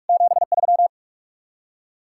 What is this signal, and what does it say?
Morse code, heard as a keyed tone, about 59 wpm, spaced as for 52 wpm, 700 Hz. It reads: 73
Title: Speed-Racing Common QSO Phrases - 45wpm